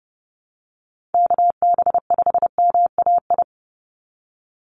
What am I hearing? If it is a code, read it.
Morse code, heard as a keyed tone, about 30 wpm, 700 Hz. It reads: KB5MAI
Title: USA Call Signs 30wpm